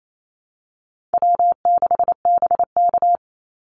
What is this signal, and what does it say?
Morse code, heard as a keyed tone, about 28 wpm, 700 Hz. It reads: W6BK
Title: Call Signs - 28wpm